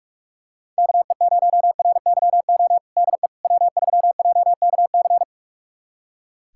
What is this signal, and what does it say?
Morse code, heard as a keyed tone, about 43 wpm, 700 Hz. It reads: KE0RYO DE W3JKC
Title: USA Callsign DE Callsign 45wpm